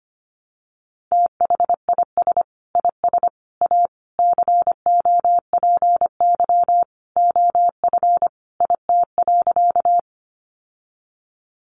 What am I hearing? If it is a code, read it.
Morse code, heard as a keyed tone, about 25 wpm, 700 Hz. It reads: THIS IS A COPY OF IT.
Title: Sentences from Top 1400 Words 25wpm